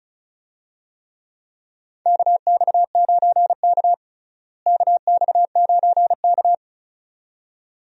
Morse code, heard as a keyed tone, about 36 wpm, 700 Hz. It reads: KX9K KX9K
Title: USA Call Signs - 4 Characters - Repeated 2x - 35wpm